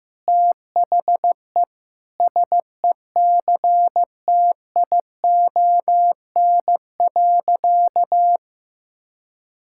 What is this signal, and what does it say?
Morse code, heard as a keyed tone, about 15 wpm, 700 Hz. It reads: THE SECTION.
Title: Sentences from Top 1000 Words - Review Part 5 - 15wpm